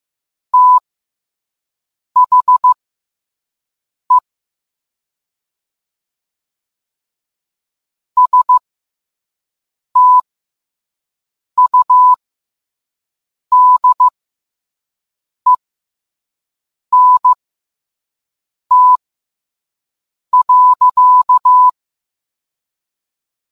Morse code, heard as a keyed tone, about 15 wpm, spaced as for 5 wpm, 1000 Hz. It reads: THE STUDENT.